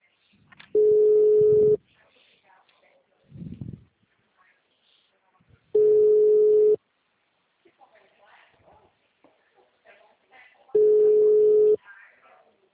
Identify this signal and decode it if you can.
Ringback tone (Europe, 425 Hz), with speech.